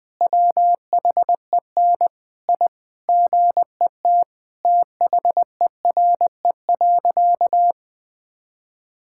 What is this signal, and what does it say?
Morse code, heard as a keyed tone, about 20 wpm, 700 Hz. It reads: WHEN I GET THERE.
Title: Sentences with 2-Word Phrases - 20wpm